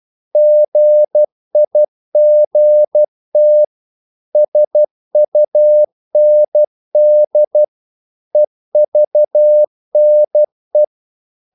Morse code, heard as a keyed tone, about 12 wpm, 600 Hz. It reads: GIGT SUND EVNE
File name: Korte ord DK 12wpm